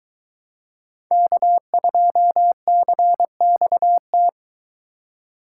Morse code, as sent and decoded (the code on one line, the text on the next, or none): -.- ..--- -.-. -..- -
K2CXT